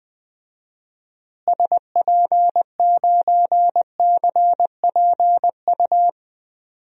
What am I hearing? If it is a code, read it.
Morse code, heard as a keyed tone, about 21 wpm, 700 Hz. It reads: SP9CPU
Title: DXCC-100 Call Signs - 6 Characters - Repeated 3x - 20wpm